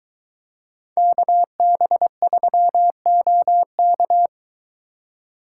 Morse code, heard as a keyed tone, about 22 wpm, 700 Hz. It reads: KB3OK